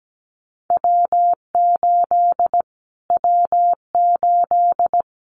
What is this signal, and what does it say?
Morse code, heard as a keyed tone, about 17 wpm, 700 Hz. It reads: W8 W8
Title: Callsign Prefixes for Top 25 DXCC - Repeated 3x - 17wpm